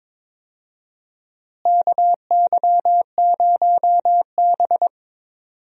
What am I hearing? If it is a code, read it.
Morse code, heard as a keyed tone, about 22 wpm, 700 Hz. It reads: KY0B
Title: USA Call Signs - 4 Characters - Repeated 3x - 22wpm